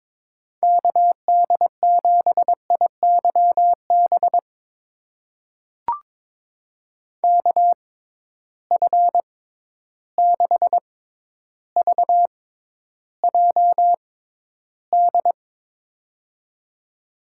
Morse code, heard as a keyed tone, about 23 wpm, spaced as for 11 wpm, 700 Hz. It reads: KD7IYB KF6VJD
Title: USA Call Signs - 6 Characters - 6x Character Spacing - 22wpm